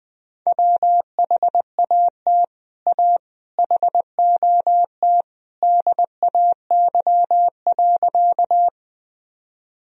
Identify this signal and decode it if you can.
Morse code, heard as a keyed tone, about 20 wpm, 700 Hz. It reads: WHAT A HOT DAY.
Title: Sentences from Top 1000 Words - Review Part 3 - 20wpm